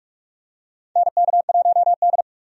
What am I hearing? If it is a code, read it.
Morse code, heard as a keyed tone, about 43 wpm, 700 Hz. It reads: NK1D
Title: USA Call Signs - 4 Characters - 45wpm